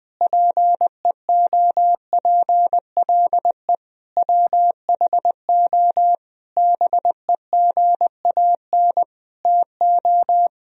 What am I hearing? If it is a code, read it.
Morse code, heard as a keyed tone, about 20 wpm, 700 Hz. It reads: PEOPLE WHO BEGAN TO
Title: Sentences from Top 1000 Words - Review Part 1 - 20wpm